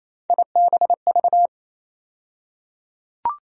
Morse code, heard as a keyed tone, about 28 wpm, 700 Hz. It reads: IBV
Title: USA Call Sign Postfixes - 3 Letters 28wpm